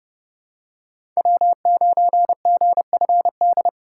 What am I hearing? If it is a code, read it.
Morse code, heard as a keyed tone, about 31 wpm, 700 Hz. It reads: W9GFD
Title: USA Call Signs - 5 Characters - 8x Character Spacing - 30wpm